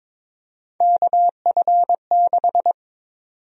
Morse code, heard as a keyed tone, about 22 wpm, 700 Hz. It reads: KF6
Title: USA Call Sign Prefixes - Repeated 2x - 22wpm